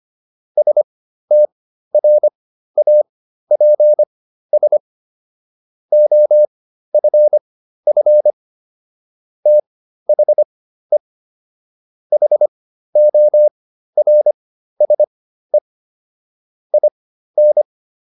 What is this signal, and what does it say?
Morse code, heard as a keyed tone, about 25 wpm, spaced as for 13 wpm, 600 Hz. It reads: STRAPS OFF THE HORSE IN